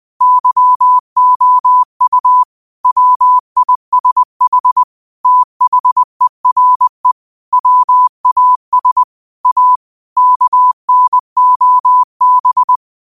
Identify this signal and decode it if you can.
Morse code, heard as a keyed tone, about 20 wpm, 1000 Hz. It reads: YOU WISH THERE WAS A KNOB
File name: qotdfiles / 20wpm